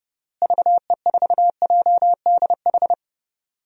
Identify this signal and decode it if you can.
Morse code, heard as a keyed tone, about 32 wpm, 700 Hz. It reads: VE4JDH
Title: DXCC-100 Call Signs - 6 Characters - Repeated 2x - 30wpm